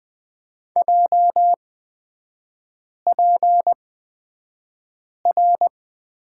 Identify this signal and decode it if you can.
Morse code, heard as a keyed tone, about 20 wpm, spaced as for 5 wpm, 700 Hz. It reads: JPR
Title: 236 Lesson 38e - Letter q - Three Characters 20wpm